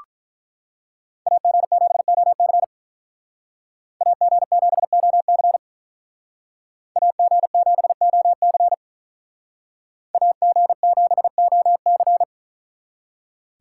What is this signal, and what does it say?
Morse code, heard as a keyed tone, about 43 wpm, 700 Hz. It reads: AG7OC AG7OC AG7OC AG7OC
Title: Speed Racing - USA 2x2 Call Signs 35wpm